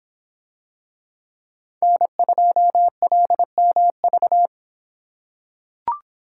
Morse code, heard as a keyed tone, about 26 wpm, 700 Hz. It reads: N2LMV